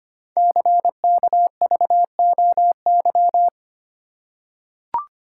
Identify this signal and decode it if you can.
Morse code, heard as a keyed tone, about 25 wpm, 700 Hz. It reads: CKVOY
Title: Code Groups - 5 Letters - 25wpm